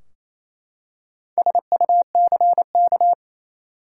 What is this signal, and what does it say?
Morse code, heard as a keyed tone, about 27 wpm, 700 Hz. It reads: SUCK
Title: Four Letter Words - 28wpm